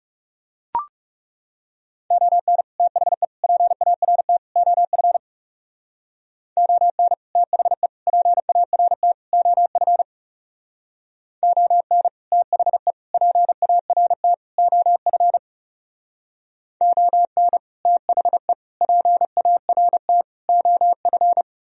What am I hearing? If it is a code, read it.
Morse code, heard as a keyed tone, about 40 wpm, 700 Hz. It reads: ON THE PART OF ON THE PART OF ON THE PART OF ON THE PART OF